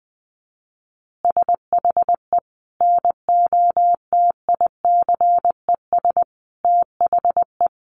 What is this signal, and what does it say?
Morse code, heard as a keyed tone, about 20 wpm, 700 Hz. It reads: SHE NOTICES THE
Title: Additional Easy Phrases 20wpm